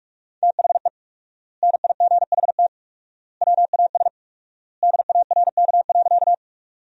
Morse code, heard as a keyed tone, about 43 wpm, 700 Hz. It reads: THE NIGHT WAS DARK.
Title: 4x Word-Spacing - Sentences from Top 500 Words - 45wpm